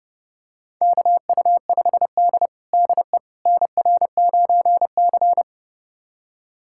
Morse code, heard as a keyed tone, about 30 wpm, 700 Hz. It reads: KU5D DE NR9C
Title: USA Callsign DE Callsign - 4 Characters - 30wpm